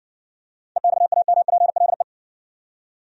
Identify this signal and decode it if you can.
Morse code, heard as a keyed tone, about 60 wpm, 700 Hz. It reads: EXAMPLE